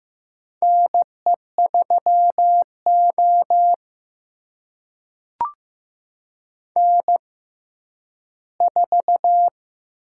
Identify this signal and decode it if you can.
Morse code, heard as a keyed tone, about 15 wpm, spaced as for 10 wpm, 700 Hz. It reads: NE3O N4
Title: USA Call Signs - 4 Characters - 6x Character Spacing - 15wpm